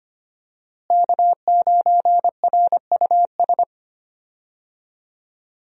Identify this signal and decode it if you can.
Morse code, heard as a keyed tone, about 25 wpm, 700 Hz. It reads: K9RUS
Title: USA Call Signs - 5 Characters - Repeated 3x - 25wpm